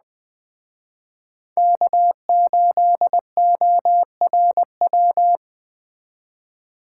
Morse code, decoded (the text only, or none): K8ORW